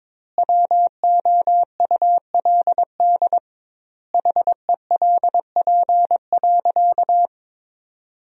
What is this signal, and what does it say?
Morse code, heard as a keyed tone, about 22 wpm, 700 Hz. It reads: WOULD HELP.
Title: 2x Word-Spacing - Sentences from Top 200 Words - 22wpm